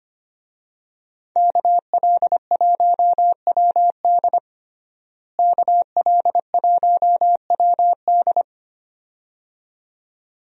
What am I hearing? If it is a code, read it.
Morse code, heard as a keyed tone, about 25 wpm, 700 Hz. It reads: KL1WD KL1WD
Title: USA Call Signs - 5 Characters - Repeated 2x - 25wpm